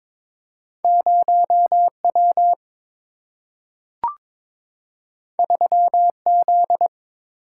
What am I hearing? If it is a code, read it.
Morse code, heard as a keyed tone, about 22 wpm, 700 Hz. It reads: 0 W 3 Z
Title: Single-Letter Single-Number 22wpm